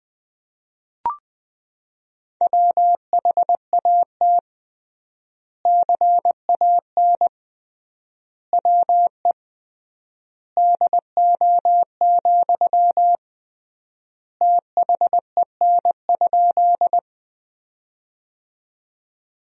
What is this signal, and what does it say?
Morse code, heard as a keyed tone, about 20 wpm, 700 Hz. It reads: WHAT CAN WE DO, THEN?
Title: 3x Word-Spacing - Sentences from Top 100 Words - 20wpm